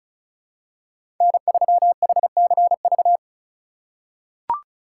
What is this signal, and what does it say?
Morse code, heard as a keyed tone, about 34 wpm, 700 Hz. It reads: N3HCV
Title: USA Call Signs - 5 Characters - 3x Character Spacing - 35wpm